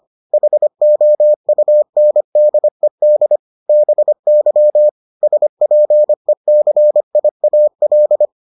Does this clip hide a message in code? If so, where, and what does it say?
Yes: Morse code from 0.3 to 8.3 s, heard as a keyed tone, about 25 wpm, 600 Hz. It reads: HOUNDED BY SPECIAL